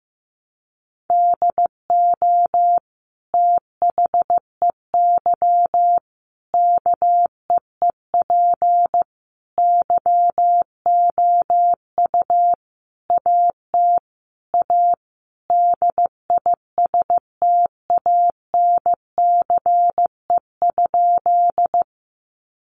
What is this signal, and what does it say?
Morse code, heard as a keyed tone, about 15 wpm, 700 Hz. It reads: DO THEY KEEP YOU AT A DISTANCE?
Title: Sentences from Top 1200 Words 15wpm